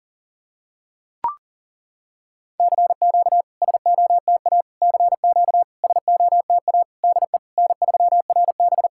Morse code, heard as a keyed tone, about 37 wpm, 690 Hz. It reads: CQ SOTA CQ SOTA DE N3RB
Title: SOTA QSO Practice 40wpm